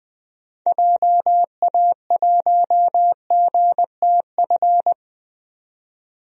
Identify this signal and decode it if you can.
Morse code, heard as a keyed tone, about 20 wpm, 700 Hz. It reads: JA1GTF
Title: DXCC-100 Call Signs - 6 Characters - 20wpm